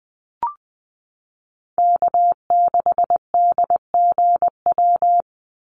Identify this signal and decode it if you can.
Morse code, heard as a keyed tone, about 19 wpm, 700 Hz. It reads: K6DGW
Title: Call Signs Repeated 3x - 20wpm